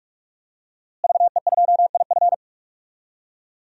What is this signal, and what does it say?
Morse code, heard as a keyed tone, about 46 wpm, 700 Hz. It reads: VE2IF
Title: DXCC-100 Call Signs - 5 Characters - Repeated 3x - 45wpm